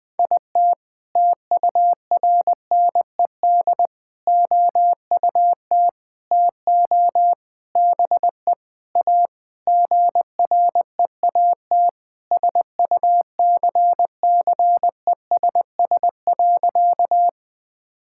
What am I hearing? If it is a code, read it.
Morse code, heard as a keyed tone, about 21 wpm, 700 Hz. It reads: IT TURNED OUT TO BE A GREAT SUCCESS.
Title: Sentences with 6-Word Phrases - Spoken Beginning - 20wpm